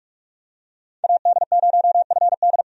Morse code, heard as a keyed tone, about 46 wpm, 700 Hz. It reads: AD0FD
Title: USA Call Signs - 5 Characters - 45wpm